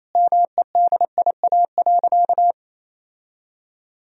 Morse code, heard as a keyed tone, about 28 wpm, 700 Hz. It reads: MEDIA.
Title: Sentences from Top 1000 Words - Review Part 4 - 28wpm